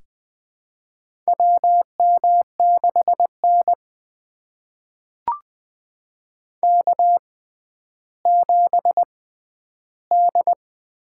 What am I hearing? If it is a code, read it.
Morse code, heard as a keyed tone, about 20 wpm, spaced as for 11 wpm, 700 Hz. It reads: WM6N K7D
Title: USA Call Signs - 4 Characters - 6x Character Spacing - 20wpm